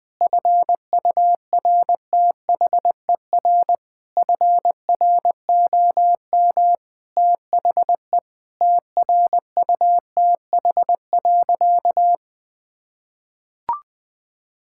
Morse code, heard as a keyed tone, about 21 wpm, 700 Hz. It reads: FURTHER FROM THE TRUTH.
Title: Sentences with 6-Word Phrases - 20wpm